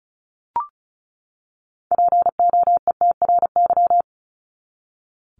Morse code, heard as a keyed tone, about 34 wpm, 700 Hz. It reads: POETRY
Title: Top 1751-2000 Words 35wpm